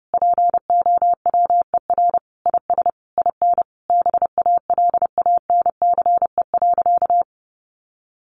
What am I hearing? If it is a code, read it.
Morse code, heard as a keyed tone, about 30 wpm, 700 Hz. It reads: POWER IS IN BALANCE.
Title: Sentences from Top 900 Words 30wpm